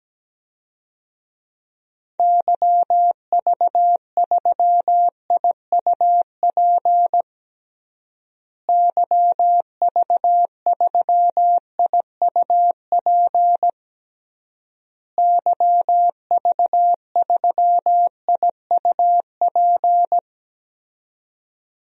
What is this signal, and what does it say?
Morse code, heard as a keyed tone, about 17 wpm, 700 Hz. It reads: YV3IUP YV3IUP YV3IUP
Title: DXCC-100 Call Signs - 6 Characters - Repeated 3x - 17wpm